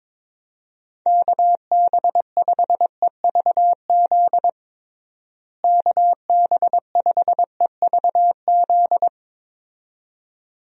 Morse code, heard as a keyed tone, about 22 wpm, 700 Hz. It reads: KB5EVZ KB5EVZ
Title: USA Call Signs - 6 Characters - Repeated 2x - 22wpm